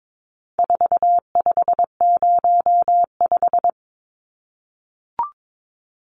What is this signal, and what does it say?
Morse code, heard as a keyed tone, about 22 wpm, 700 Hz. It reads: 4505
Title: Numbers - 4-Digit 22wpm